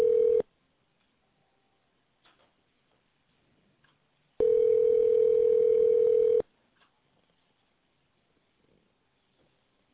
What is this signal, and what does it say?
Ringback tone (North America), 440 + 480 Hz.